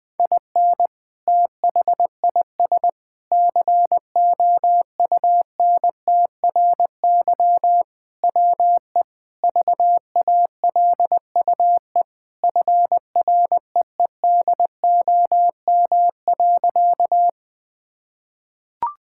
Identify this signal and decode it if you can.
Morse code, heard as a keyed tone, about 20 wpm, 700 Hz. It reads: IN THIS COUNTRY WE VALUE FREEDOM.
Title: Sentences with 3-Word Phrases - Spoken Beginning - 20wpm